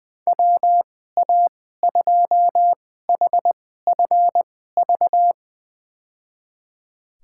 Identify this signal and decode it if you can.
Morse code, heard as a keyed tone, about 20 wpm, spaced as for 14 wpm, 700 Hz. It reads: WA2HFV
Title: USA Call Signs - 6 Characters - 2x Character Spacing - 20wpm